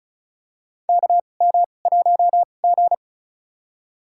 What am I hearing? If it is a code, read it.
Morse code, heard as a keyed tone, about 36 wpm, spaced as for 25 wpm, 700 Hz. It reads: KM1G
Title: USA Call Signs - 2x Character Spacing - 35wpm